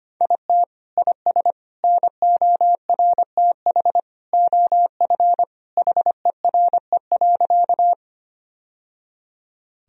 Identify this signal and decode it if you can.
Morse code, heard as a keyed tone, about 25 wpm, 700 Hz. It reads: IT IS NORTH OF HERE.
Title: Sentences from Top 1000 Words - Review Part 1 - 25wpm